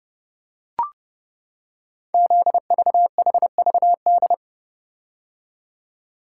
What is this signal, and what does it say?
Morse code, heard as a keyed tone, about 28 wpm, 700 Hz. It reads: ZVHVD
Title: Code Groups - 5 Letters - 30wpm